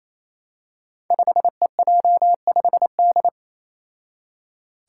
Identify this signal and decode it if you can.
Morse code, heard as a keyed tone, about 27 wpm, 700 Hz. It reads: 5EJ5D
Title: Five Challenging Letter-Number Combinations 28wpm